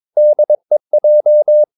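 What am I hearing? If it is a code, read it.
Morse code, heard as a keyed tone, about 22 wpm, 600 Hz. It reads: DEJ
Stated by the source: Morse practice of short Danish words at 22 wpm, from CW med Gnister.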